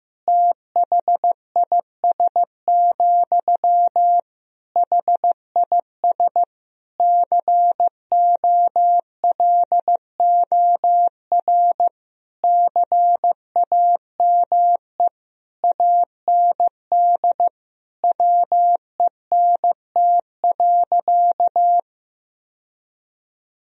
Morse code, heard as a keyed tone, about 16 wpm, 700 Hz. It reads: THIS, HIS COLOR CAME AND WENT.
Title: Sentences from Top 1000 Words - Review Part 2 - 15wpm